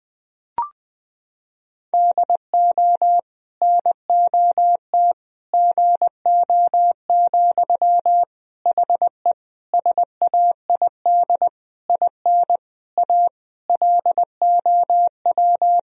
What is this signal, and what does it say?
Morse code, heard as a keyed tone, about 20 wpm, 700 Hz. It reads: DO NOT GO, HE SAID IN A LOW
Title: Sentences from Top 1000 Words - Review Part 1 - 20wpm